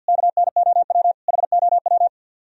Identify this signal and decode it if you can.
Morse code, heard as a keyed tone, about 64 wpm, 700 Hz. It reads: KNOW HOW